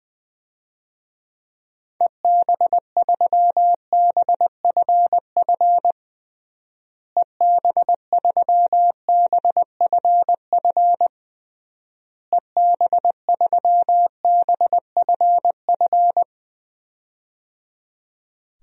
Morse code, heard as a keyed tone, about 20 wpm, 700 Hz. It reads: EB3BFF EB3BFF EB3BFF
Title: DXCC-100 Call Signs - 6 Characters - Repeated 3x - 20wpm